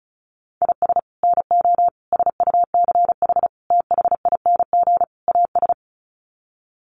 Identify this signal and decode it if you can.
Morse code, heard as a keyed tone, about 34 wpm, 700 Hz. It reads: IS NO SUCH THING AS
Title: Sets of 5 Words Encore 35wpm